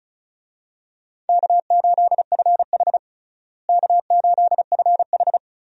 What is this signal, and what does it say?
Morse code, heard as a keyed tone, about 36 wpm, 700 Hz. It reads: K8FH K8FH
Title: USA Call Signs - 4 Characters - Repeated 3x - 35wpm